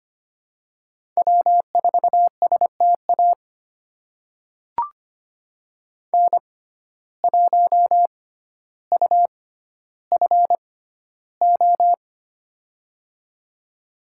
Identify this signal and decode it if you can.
Morse code, heard as a keyed tone, about 25 wpm, spaced as for 13 wpm, 700 Hz. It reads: W4STA N1UFO
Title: USA Call Signs - 5 Characters - 6x Character Spacing - 25wpm